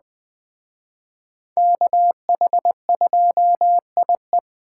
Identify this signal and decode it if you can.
Morse code, heard as a keyed tone, about 21 wpm, 700 Hz. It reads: KH2IE
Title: Speed Racing - USA 2x2 Call Signs 17wpm